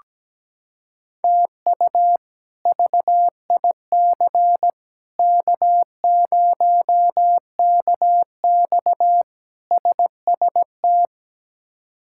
Morse code, heard as a keyed tone, about 17 wpm, 700 Hz. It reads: TU VIC K0KX SST
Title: K1USN SST QSO Practice - 17wpm